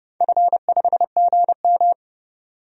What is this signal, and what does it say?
Morse code, heard as a keyed tone, about 30 wpm, 700 Hz. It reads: F5GM
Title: DXCC-100 Call Signs - 4 Characters - 30wpm